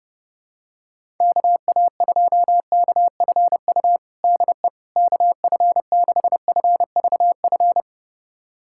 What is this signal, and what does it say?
Morse code, heard as a keyed tone, about 30 wpm, 700 Hz. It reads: KA2KFU DE KF6FVF
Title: USA Callsign DE Callsign - 6 Characters - 30wpm